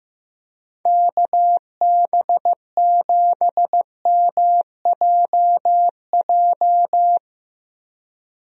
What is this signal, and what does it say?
Morse code, heard as a keyed tone, about 15 wpm, 700 Hz. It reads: KB7MJJ
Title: USA Call Signs - 6 Characters - Repeated 3x - 15wpm